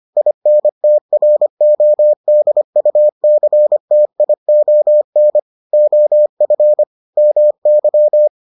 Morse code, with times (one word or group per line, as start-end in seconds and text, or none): 0.2-5.4 INTRODUCTION
5.7-6.8 OF
7.2-8.3 MY